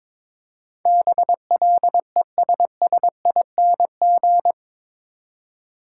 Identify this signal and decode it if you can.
Morse code, heard as a keyed tone, about 22 wpm, 700 Hz. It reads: BLESSING
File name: practice / 22wpm / Eight Letter Words - 22wpm